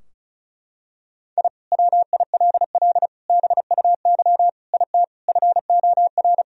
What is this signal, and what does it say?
Morse code, heard as a keyed tone, about 34 wpm, 700 Hz. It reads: I WILL BUY IT FOR
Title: Sentences from Top 1000 Words - Review Part 4 - 35wpm